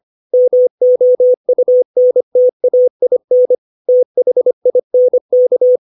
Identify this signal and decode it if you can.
Morse code, heard as a keyed tone, about 25 wpm, 500 Hz. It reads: MOUNTAIN THINK
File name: 632 words / 632 Words 25WPM